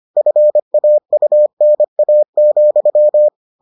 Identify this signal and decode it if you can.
Morse code, heard as a keyed tone, about 25 wpm, 600 Hz. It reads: FAUNA,